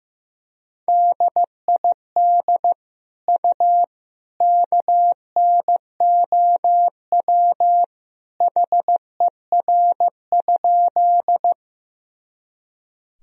Morse code, heard as a keyed tone, about 15 wpm, 700 Hz. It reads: DID U KNOW HER?
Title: Sentences from Top 100 Words and Abbreviations 15wpm